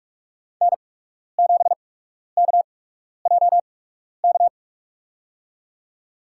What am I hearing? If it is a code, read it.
Morse code, heard as a keyed tone, about 46 wpm, spaced as for 12 wpm, 700 Hz. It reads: N7KJK
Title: USA Call Signs - 5 Characters - 8x Character Spacing - 45wpm